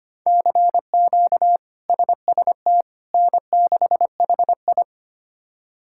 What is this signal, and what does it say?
Morse code, heard as a keyed tone, about 25 wpm, 700 Hz. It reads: CQ SST N6HI
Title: K1USN SST QSO Practice - 25wpm